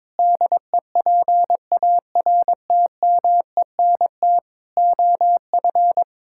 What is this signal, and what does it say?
Morse code, heard as a keyed tone, about 22 wpm, 700 Hz. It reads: DEPARTMENT OF
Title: Sentences from Top 1000 Words - Review Part 5 - 22wpm